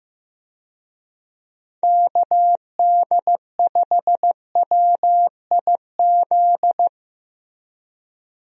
Morse code, heard as a keyed tone, about 15 wpm, 700 Hz. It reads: KD5WIZ
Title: USA Call Signs - 6 Characters - Repeated 2x - 15wpm